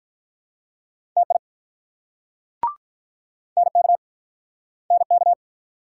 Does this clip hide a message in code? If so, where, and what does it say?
Yes: Morse code from 1.2 to 5.3 s, heard as a keyed tone, about 48 wpm, 700 Hz. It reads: TI NK NK